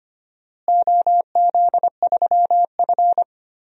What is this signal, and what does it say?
Morse code, heard as a keyed tone, about 25 wpm, 700 Hz. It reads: OZ3F
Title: DXCC-100 Call Signs - 4 Characters - Repeated 3x - 25wpm